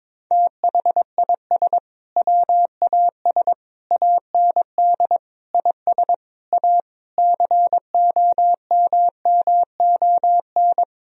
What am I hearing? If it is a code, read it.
Morse code, heard as a keyed tone, about 22 wpm, 700 Hz. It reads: THIS WAS AND IS A COMMON
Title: Sentences from Top 1000 Words - Review Part 4 - 22wpm